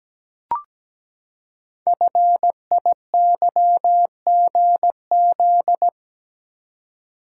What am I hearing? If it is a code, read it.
Morse code, heard as a keyed tone, about 17 wpm, 700 Hz. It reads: FIYGZ
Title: Code Groups - 5 Letters - 17wpm